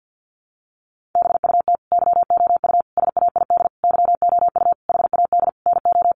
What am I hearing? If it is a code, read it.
Morse code, heard as a keyed tone, about 50 wpm, 700 Hz. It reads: BUT YOU SAID YOU HAD NO
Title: Sentences from Top 1400 Words 50wpm